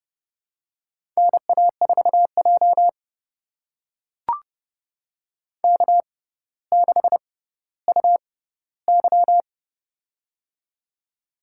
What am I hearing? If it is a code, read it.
Morse code, heard as a keyed tone, about 30 wpm, spaced as for 15 wpm, 700 Hz. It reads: NA4J K6UY
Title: USA Call Signs - 4 Characters - 6x Character Spacing - 30wpm